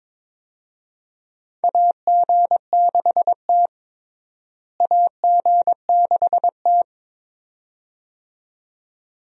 Morse code, heard as a keyed tone, about 22 wpm, 700 Hz. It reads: AG6T AG6T
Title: USA Call Signs - 4 Characters - Repeated 2x - 22wpm